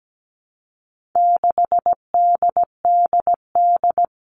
Morse code, heard as a keyed tone, about 17 wpm, 700 Hz. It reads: 6DDD
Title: All Cut Numbers - 4-Digit 17wpm